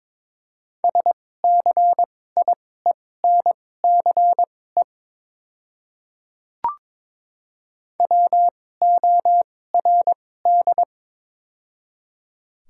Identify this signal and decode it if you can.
Morse code, heard as a keyed tone, about 22 wpm, spaced as for 16 wpm, 700 Hz. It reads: SCIENCE WORD
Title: 2x Character Spacing - Top 500 Words - 22wpm